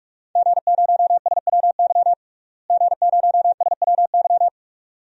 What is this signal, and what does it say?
Morse code, heard as a keyed tone, about 46 wpm, 700 Hz. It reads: G0SWY G0SWY
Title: DXCC-100 Call Signs - 5 Characters - Repeated 2x - 45wpm